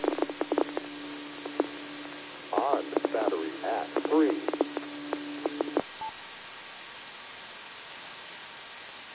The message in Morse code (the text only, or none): none